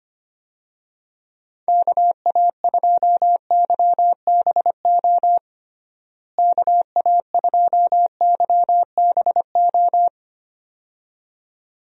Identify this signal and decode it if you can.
Morse code, heard as a keyed tone, about 25 wpm, 700 Hz. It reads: KA2YBO KA2YBO
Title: USA Call Signs - 6 Characters - Repeated 2x - 25wpm